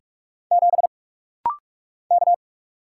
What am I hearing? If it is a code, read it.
Morse code, heard as a keyed tone, about 44 wpm, 700 Hz. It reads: 7 K
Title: Single Letter-Number - Rapid-Fire 45wpm